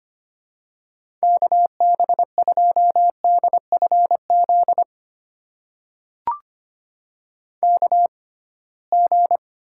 Morse code, heard as a keyed tone, about 25 wpm, spaced as for 19 wpm, 700 Hz. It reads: KB2DFZ KG
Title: USA Call Signs - 6 Characters - 6x Character Spacing - 25wpm